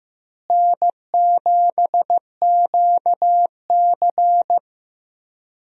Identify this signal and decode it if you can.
Morse code, heard as a keyed tone, about 15 wpm, 700 Hz. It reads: N7QC
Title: USA Call Signs - 4 Characters - Repeated 2x - 15wpm